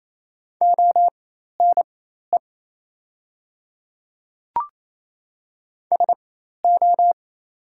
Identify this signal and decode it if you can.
Morse code, heard as a keyed tone, about 27 wpm, spaced as for 13 wpm, 700 Hz. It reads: ONE SO
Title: 4x Character Spacing - Top 100 Words - 28wpm